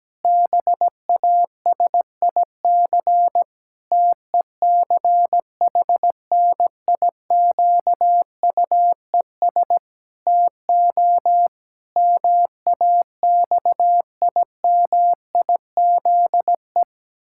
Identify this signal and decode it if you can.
Morse code, heard as a keyed tone, about 17 wpm, 700 Hz. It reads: BASIC TECHNIQUES TO MAXIMIZE
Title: Pangram Challenge v2 - 17wpm